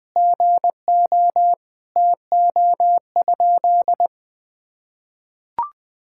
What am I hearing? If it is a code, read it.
Morse code, heard as a keyed tone, about 20 wpm, 700 Hz. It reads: GO TO?
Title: Sentences from Top 1000 Words - Review Part 5 - 20wpm